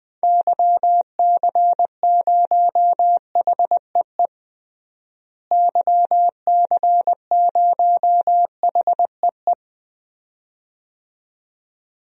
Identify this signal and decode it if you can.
Morse code, heard as a keyed tone, about 21 wpm, 700 Hz. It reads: YC0HEE YC0HEE
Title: DXCC-100 Call Signs - 6 Characters - Repeated 2x - 20wpm